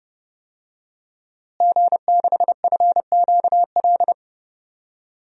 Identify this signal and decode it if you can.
Morse code, heard as a keyed tone, about 29 wpm, 700 Hz. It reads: G6FQL
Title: DXCC-100 Call Signs - 5 Characters - Repeated 2x - 30wpm